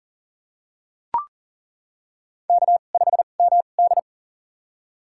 Morse code, heard as a keyed tone, about 40 wpm, spaced as for 29 wpm, 700 Hz. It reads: K5MD
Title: USA Call Signs - 4 Characters - 2x Character Spacing - 40wpm